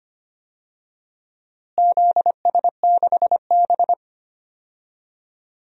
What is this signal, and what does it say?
Morse code, heard as a keyed tone, about 25 wpm, 700 Hz. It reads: ZS6B
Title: DXCC-100 Call Signs - 4 Characters - Repeated 3x - 25wpm